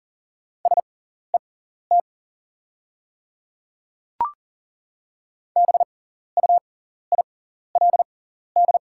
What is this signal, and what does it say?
Morse code, heard as a keyed tone, about 40 wpm, spaced as for 14 wpm, 690 Hz. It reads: SET BUILD
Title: 6x Character Spacing - Top 200 Words - 40wpm